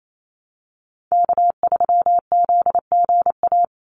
Morse code, heard as a keyed tone, about 28 wpm, 700 Hz. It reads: K3ZGA
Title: Call Signs Repeated 3x - 28wpm